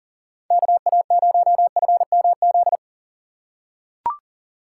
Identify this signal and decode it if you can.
Morse code, heard as a keyed tone, about 40 wpm, 700 Hz. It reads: KA0FMZ
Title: USA Call Signs - 6 Characters - 4x Character Spacing - 40wpm